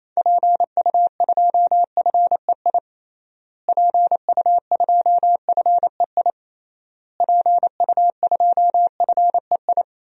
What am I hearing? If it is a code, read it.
Morse code, heard as a keyed tone, about 28 wpm, 700 Hz. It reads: PU2FEI PU2FEI PU2FEI
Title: DXCC-100 Call Signs - 6 Characters - Repeated 3x - 28wpm